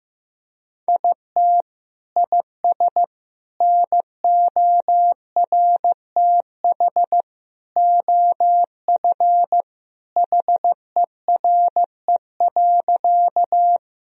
Morse code, heard as a keyed tone, about 15 wpm, 700 Hz. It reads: IT IS NORTH OF HERE.
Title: Sentences from Top 1000 Words - Review Part 1 - 15wpm